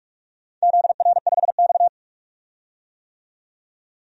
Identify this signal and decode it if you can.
Morse code, heard as a keyed tone, about 44 wpm, 700 Hz. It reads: ZR5X
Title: DXCC-100 Call Signs - 4 Characters - 45wpm